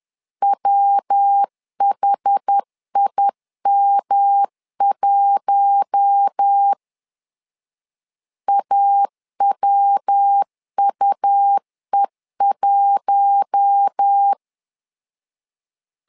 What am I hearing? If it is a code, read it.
Morse code, heard as a keyed tone, about 10 wpm, 800 Hz. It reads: WHIM1 AWUE1
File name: LEZIONI / 4 Q1 10wpm 2448